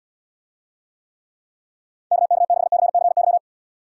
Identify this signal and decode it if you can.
Morse code, heard as a keyed tone, about 73 wpm, 700 Hz. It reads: KK6CCY